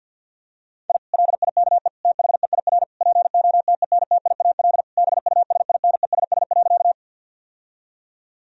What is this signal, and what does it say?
Morse code, heard as a keyed tone, about 50 wpm, 700 Hz. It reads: I LIKE THEIR POTENTIAL BUSINESS.
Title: Sentences from Top 1000 Words - Review Part 4 - 50wpm